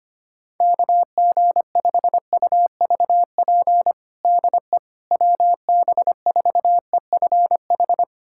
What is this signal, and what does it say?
Morse code, heard as a keyed tone, about 25 wpm, 700 Hz. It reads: KG5UVP DE WB4EFH
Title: USA Callsign DE Callsign 25wpm